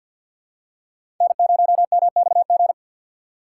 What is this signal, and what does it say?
Morse code, heard as a keyed tone, about 50 wpm, 700 Hz. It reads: N0MXG
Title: USA Call Signs - 5 Characters - 6x Character Spacing - 50wpm